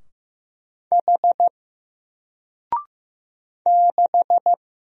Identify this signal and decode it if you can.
Morse code, heard as a keyed tone, about 15 wpm, 700 Hz. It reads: H 6
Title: Single Character (Letter:Number:Punctuation:Prosign) 15wpm